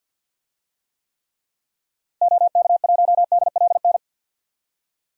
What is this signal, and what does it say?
Morse code, heard as a keyed tone, about 52 wpm, 700 Hz. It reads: OK1DLN
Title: DXCC-100 Call Signs - 6 Characters - Repeated 3x - 50wpm